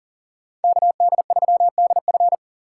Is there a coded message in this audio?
Yes: Morse code from 0.6 to 2.4 s, heard as a keyed tone, about 50 wpm, 700 Hz. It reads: KD3DF